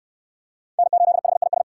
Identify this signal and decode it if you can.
Morse code, heard as a keyed tone, about 64 wpm, 700 Hz. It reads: N8HIS